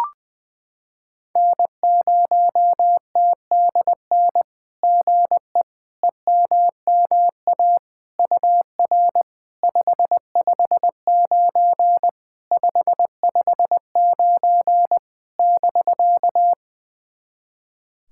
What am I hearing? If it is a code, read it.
Morse code, heard as a keyed tone, about 20 wpm, 700 Hz. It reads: N0TDN GE EMMA UR 559 559 <BK>
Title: SOTA QSO Practice 20wpm